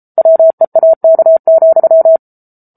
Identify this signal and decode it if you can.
Morse code, heard as a keyed tone, about 34 wpm, 650 Hz. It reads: WEAK,
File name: Book Two / 07-Chapter 7 - 33 WPM